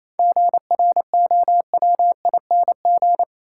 Morse code, heard as a keyed tone, about 28 wpm, 700 Hz. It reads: GROWING
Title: Sets of 5 Words Encore - New Word Drill - 28wpm